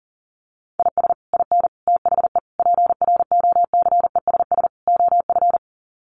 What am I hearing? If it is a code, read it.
Morse code, heard as a keyed tone, about 37 wpm, 700 Hz. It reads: IS IN THE PROCESS OF
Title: Sets of 5 Words Encore2 40wpm